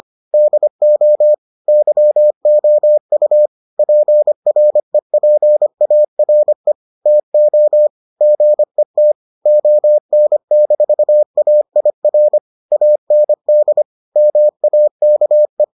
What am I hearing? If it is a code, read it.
Morse code, heard as a keyed tone, about 25 wpm, 600 Hz. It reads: DO YOU PREPARE TO GET ON-AIR AND MAKE